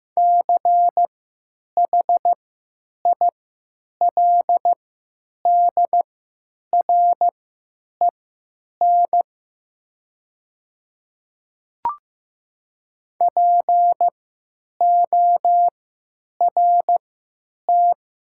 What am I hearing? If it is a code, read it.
Morse code, heard as a keyed tone, about 15 wpm, spaced as for 9 wpm, 700 Hz. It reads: CHILDREN PORT
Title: 3x Character Spacing - Top 300 Words - 15wpm